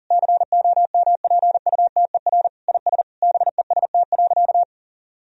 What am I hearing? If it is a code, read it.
Morse code, heard as a keyed tone, about 40 wpm, 700 Hz. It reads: COMPUTER IS BEST.
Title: Sentences from Top 1000 Words - Review Part 3 - 40wpm